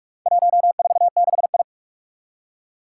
Morse code, heard as a keyed tone, about 41 wpm, 700 Hz. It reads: 146I
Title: Four Challenging Letter-Number Combinations 45wpm